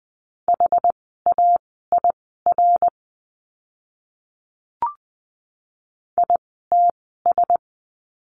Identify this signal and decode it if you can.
Morse code, heard as a keyed tone, about 20 wpm, spaced as for 14 wpm, 700 Hz. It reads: HAIR ITS
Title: 101 Lesson 17f - Letter u - Words 20wpm